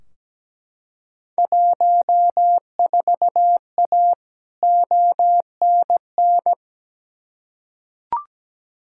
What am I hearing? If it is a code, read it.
Morse code, heard as a keyed tone, about 17 wpm, 700 Hz. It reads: 14A ONN
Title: Field Day Exchanges Encore - 17wpm